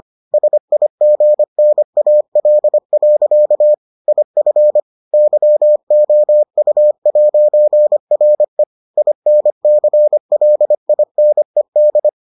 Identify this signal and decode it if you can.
Morse code, heard as a keyed tone, about 25 wpm, 600 Hz. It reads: SIGNAL. IF YOU'RE INCLINED